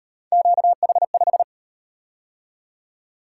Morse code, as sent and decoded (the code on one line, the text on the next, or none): --.- .... .....
QH5